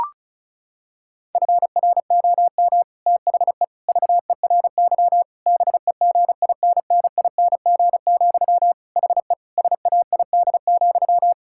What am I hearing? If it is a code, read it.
Morse code, heard as a keyed tone, about 34 wpm, 700 Hz. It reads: FROM THE VERY BEGINNING, HE SAID,
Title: Sentences from Top 1600 Words 35wpm